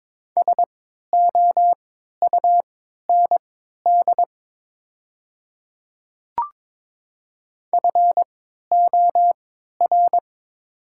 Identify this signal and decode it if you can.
Morse code, heard as a keyed tone, about 22 wpm, spaced as for 12 wpm, 700 Hz. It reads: SOUND FOR
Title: 3x Character Spacing - Top 100 Words - 22wpm